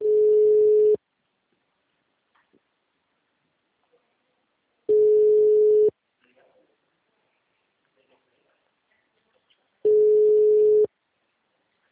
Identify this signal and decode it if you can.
Ringback tone (Europe, 425 Hz), with speech.
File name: out- -2012- - - .456